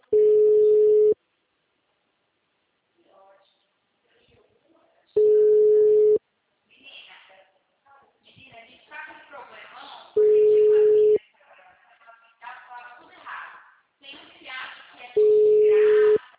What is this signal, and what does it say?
Ringback tone (Europe, 425 Hz), with speech.
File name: out- -2033- - - .78276